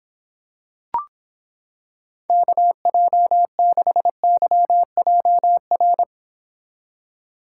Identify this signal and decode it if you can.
Morse code, heard as a keyed tone, about 26 wpm, 700 Hz. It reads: KJ6YJR